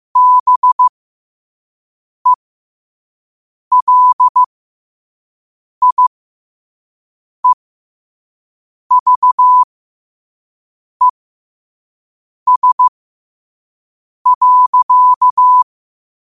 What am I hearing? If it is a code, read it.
Morse code, heard as a keyed tone, about 15 wpm, spaced as for 5 wpm, 1000 Hz. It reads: BELIEVES.
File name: qotdfiles / 5wpm